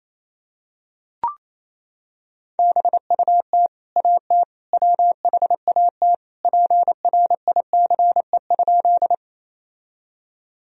Morse code, heard as a keyed tone, about 28 wpm, 700 Hz. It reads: BUT AT WHAT PRICE?
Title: Sentences from Top 1000 Words - Review Part 4 - 28wpm